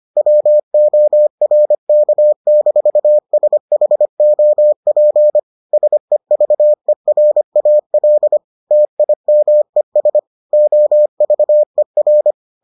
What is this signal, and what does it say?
Morse code, heard as a keyed tone, about 25 wpm, 600 Hz. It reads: WORK-SHOP SEVERAL TIMES OVER